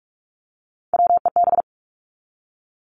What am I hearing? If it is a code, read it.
Morse code, heard as a keyed tone, about 44 wpm, 700 Hz. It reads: WEB
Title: Top 501-750 Words 45wpm